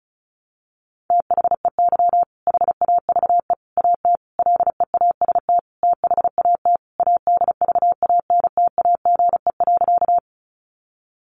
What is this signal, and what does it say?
Morse code, heard as a keyed tone, about 34 wpm, 700 Hz. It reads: THEY HAVE AT LEAST THAT ADVANTAGE.
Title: Sentences from Top 900 Words 35wpm